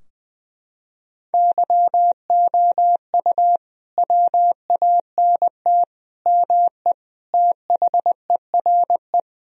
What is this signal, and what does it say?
Morse code, heard as a keyed tone, about 20 wpm, 700 Hz. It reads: YOU WANT ME THERE
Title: Sentences from Top 1000 Words - Review Part 2 - 20wpm